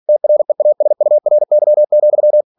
Morse code, heard as a keyed tone, about 46 wpm, 600 Hz. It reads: TREASURY,
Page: Never 47wpm